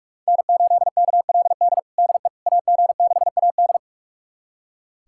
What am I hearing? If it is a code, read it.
Morse code, heard as a keyed tone, about 46 wpm, 700 Hz. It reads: N8KLD DE AG6AD
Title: USA Callsign DE Callsign - 5 Characters - 45wpm